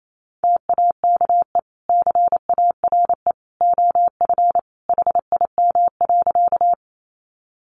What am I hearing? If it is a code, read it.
Morse code, heard as a keyed tone, about 28 wpm, 700 Hz. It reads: TAKE CARE OF HIM.
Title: Sentences from Top 300 Words 28wpm